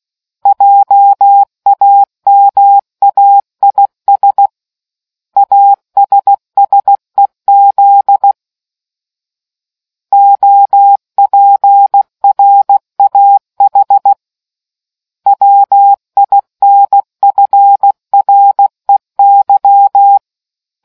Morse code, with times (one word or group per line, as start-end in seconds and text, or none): 0.5-4.5 JAMAIS
5.4-8.3 ASSEZ
10.1-14.1 OPRAH
15.3-20.2 WINFREY